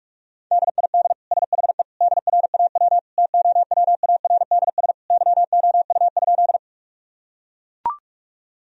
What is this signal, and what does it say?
Morse code, heard as a keyed tone, about 43 wpm, 700 Hz. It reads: DID SHE DRAW TOWARDS YOU?
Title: Sentences from Top 1000 Words - Review Part 2 - 45wpm